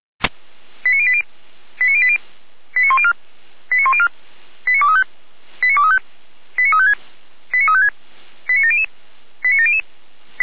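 Five-tone selective call (ZVEI), heard as quick runs of five tones, tones of about 80 ms. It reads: 89080 89080 89104 89104 89235 89235 89356 89356 89700 89700
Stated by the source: Five-tone ZVEI selective call received on an SDR.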